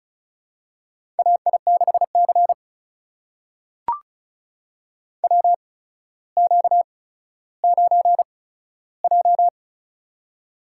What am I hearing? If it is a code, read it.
Morse code, heard as a keyed tone, about 36 wpm, spaced as for 15 wpm, 700 Hz. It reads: AI6C WQ9J
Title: USA Call Signs - 4 Characters - 8x Character Spacing - 35wpm